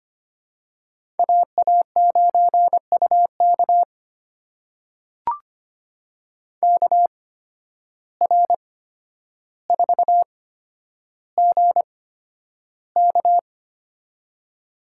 Morse code, heard as a keyed tone, about 25 wpm, spaced as for 11 wpm, 700 Hz. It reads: AA9UK KR4GK
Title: USA Call Signs - 5 Characters - 8x Character Spacing - 25wpm